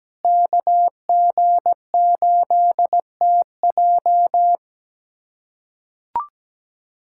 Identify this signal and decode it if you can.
Morse code, heard as a keyed tone, about 17 wpm, 700 Hz. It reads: KG8TJ
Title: USA Callsign DE Callsign - 5 Characters - 17wpm